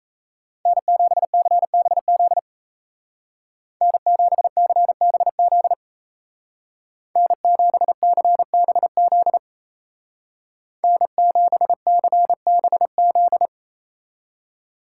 Morse code, heard as a keyed tone, about 34 wpm, 700 Hz. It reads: N7CBZ N7CBZ N7CBZ N7CBZ
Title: Speed Racing - USA Call Signs 28wpm